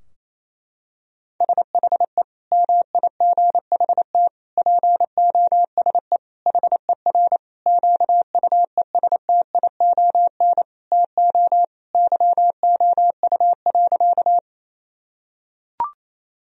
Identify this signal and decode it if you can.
Morse code, heard as a keyed tone, about 28 wpm, 700 Hz. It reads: SHE MIGHT POSE HER QUESTION TO YOU.
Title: Sentences from Top 1000 Words - Review Part 2 - 28wpm